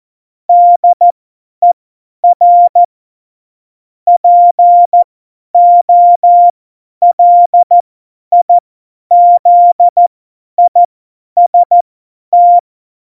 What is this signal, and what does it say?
Morse code, heard as a keyed tone, about 14 wpm, spaced as for 10 wpm, 700 Hz. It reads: DER POLIZIST
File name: Morsekurs / Witze / Witz 05. Februar 2019, 50 F70